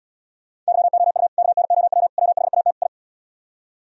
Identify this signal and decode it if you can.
Morse code, heard as a keyed tone, about 77 wpm, 700 Hz. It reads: YOU KNOW WHAT I